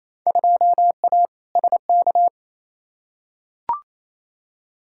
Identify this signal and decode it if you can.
Morse code, heard as a keyed tone, about 28 wpm, 700 Hz. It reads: 2A SK
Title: Field Day Exchanges Encore - 28wpm